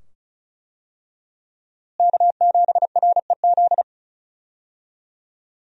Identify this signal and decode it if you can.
Morse code, heard as a keyed tone, about 36 wpm, 700 Hz. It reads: K7REZ
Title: USA Call Signs - 5 Characters - Repeated 2x - 35wpm